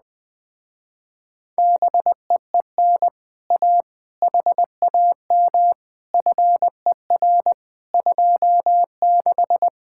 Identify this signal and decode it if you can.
Morse code, heard as a keyed tone, about 20 wpm, 700 Hz. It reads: BEEN A HAM FER 26
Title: Speed-Racing Common QSO Phrases - 17wpm